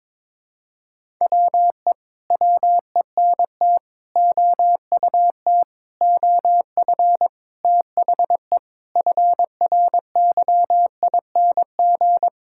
Morse code, heard as a keyed tone, about 21 wpm, 700 Hz. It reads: WE WENT OUT OF THE FRYING
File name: Sentences with 9-Word Phrases - 22wpm